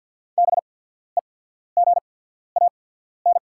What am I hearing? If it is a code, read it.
Morse code, heard as a keyed tone, about 48 wpm, spaced as for 14 wpm, 690 Hz. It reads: BEGAN
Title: 8x Character Spacing - Top 300 Words - 50wpm